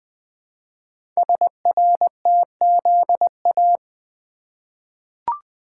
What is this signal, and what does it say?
Morse code, heard as a keyed tone, about 20 wpm, 700 Hz. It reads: SRTZA
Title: Code Groups - 5 Letters - 20wpm